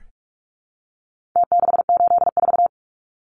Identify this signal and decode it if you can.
Morse code, heard as a keyed tone, about 44 wpm, 700 Hz. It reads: T684
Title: Numbers - Contest Style - 4-Digit 45wpm